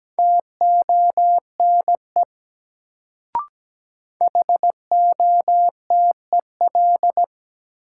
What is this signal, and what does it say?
Morse code, heard as a keyed tone, about 17 wpm, 700 Hz. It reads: TONE HOTEL
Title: Sentences from Top 1600 Words - New Word Drill Alt - 17wpm